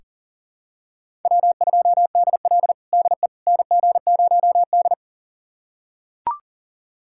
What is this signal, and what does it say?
Morse code, heard as a keyed tone, about 37 wpm, 700 Hz. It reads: W2DL DE NG0D
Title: USA Callsign DE Callsign - 4 Characters - 40wpm